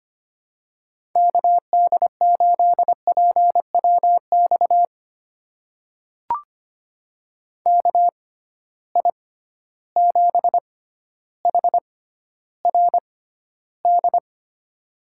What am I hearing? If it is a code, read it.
Morse code, heard as a keyed tone, about 25 wpm, spaced as for 13 wpm, 700 Hz. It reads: KD8PWX KI7HRD
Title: USA Call Signs - 6 Characters - 6x Character Spacing - 25wpm